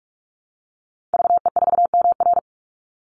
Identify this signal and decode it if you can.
Morse code, heard as a keyed tone, about 44 wpm, 700 Hz. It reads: VE4MR
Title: Call Signs - 45wpm